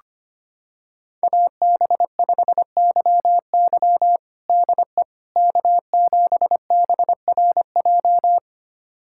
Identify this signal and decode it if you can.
Morse code, heard as a keyed tone, about 25 wpm, 700 Hz. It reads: AB5YY DE K7BRJ
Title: USA Callsign DE Callsign - 5 Characters - 25wpm